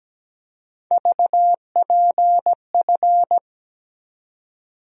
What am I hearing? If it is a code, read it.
Morse code, heard as a keyed tone, about 17 wpm, 700 Hz. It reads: VPF
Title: USA Call Sign Postfixes - 3 Letters 17wpm